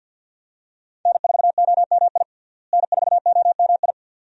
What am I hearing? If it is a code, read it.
Morse code, heard as a keyed tone, about 49 wpm, 700 Hz. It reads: N4OMI N4OMI
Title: USA Call Signs - 5 Characters - Repeated 2x - 50wpm